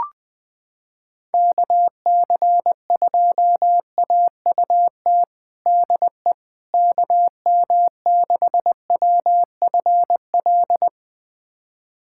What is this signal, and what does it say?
Morse code, heard as a keyed tone, about 20 wpm, 700 Hz. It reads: KC2AUT DE KM6WFL
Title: USA Callsign DE Callsign - 6 Characters - 20wpm